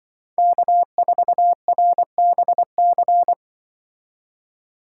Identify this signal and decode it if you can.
Morse code, heard as a keyed tone, about 24 wpm, 700 Hz. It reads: K4RBC
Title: Speed Racing - USA Call Signs 20wpm